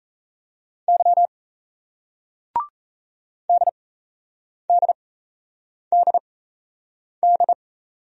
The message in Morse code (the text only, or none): Y DDDD